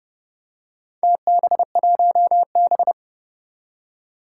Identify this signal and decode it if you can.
Morse code, heard as a keyed tone, about 30 wpm, 690 Hz. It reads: TB1B
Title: Four Challenging Letter-Number Combinations 30wpm